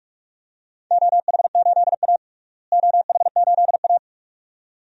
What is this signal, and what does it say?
Morse code, heard as a keyed tone, about 45 wpm, 700 Hz. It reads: OH8A OH8A
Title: DXCC-100 Call Signs - 4 Characters - Repeated 2x - 45wpm